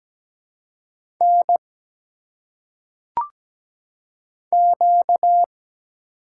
Morse code, heard as a keyed tone, about 17 wpm, 700 Hz. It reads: N Q
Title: Single Letters - Phonetic Pronunciation 17wpm